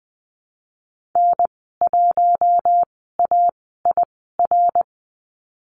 Morse code, heard as a keyed tone, about 20 wpm, spaced as for 15 wpm, 700 Hz. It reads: N1AIR
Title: 128 Lesson 21g - Number 6 - Call Signs 20wpm